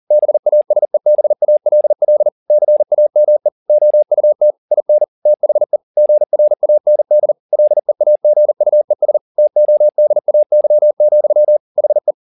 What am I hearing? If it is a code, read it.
Morse code, heard as a keyed tone, about 40 wpm, 600 Hz. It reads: BASEBALL CAME OUT IN THE GRAND LEAGUES TODAY, HE